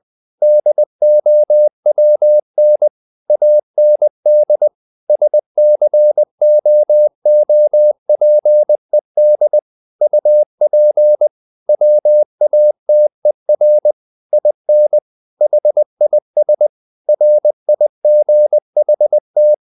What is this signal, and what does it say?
Morse code, heard as a keyed tone, about 20 wpm, 600 Hz. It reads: DOWN AND SCOOPED UP WATER IN HIS RIGHT
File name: the old man and the sea part 2 20wpm